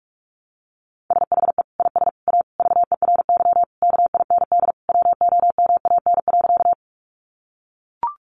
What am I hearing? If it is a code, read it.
Morse code, heard as a keyed tone, about 44 wpm, 700 Hz. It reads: SHE IS A VERY KIND WOMAN.
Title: Sentences from Top 1100 Words 45wpm